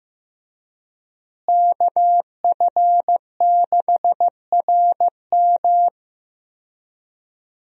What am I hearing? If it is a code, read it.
Morse code, heard as a keyed tone, about 15 wpm, 700 Hz. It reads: KF6RM
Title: USA Call Signs - 5 Characters - Repeated 3x - 15wpm